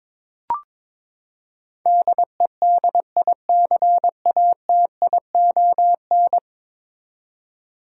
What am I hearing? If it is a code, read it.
Morse code, heard as a keyed tone, about 22 wpm, 700 Hz. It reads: DEDICATION
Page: morse-code-ninja-practice-22wpm